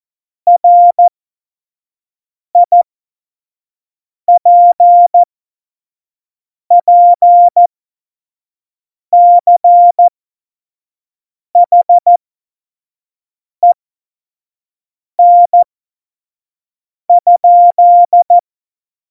Morse code, heard as a keyed tone, about 14 wpm, spaced as for 5 wpm, 700 Hz. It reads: RIPPCHEN?
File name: Morsekurs / Witze / Witz 20. August 2019, 25 F70